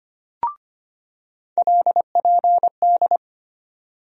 Morse code, heard as a keyed tone, about 25 wpm, 700 Hz. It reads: LPD
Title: USA Call Sign Postfixes - 3 Letters 25wpm